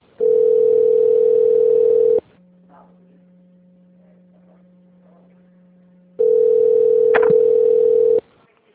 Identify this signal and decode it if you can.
Ringback tone (North America), 440 + 480 Hz.